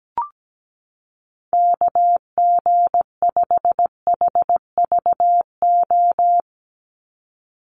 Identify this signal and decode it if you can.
Morse code, heard as a keyed tone, about 17 wpm, 700 Hz. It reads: KG5HVO
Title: Call Signs - 17wpm